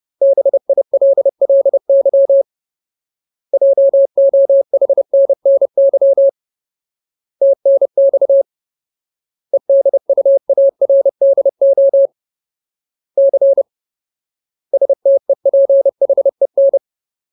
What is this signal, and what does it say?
Morse code, heard as a keyed tone, about 30 wpm, 550 Hz. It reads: BILLY JOHNNY TNX EDUARDO C STEPHEN
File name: Ultimate Word List / 30WPM Ultimate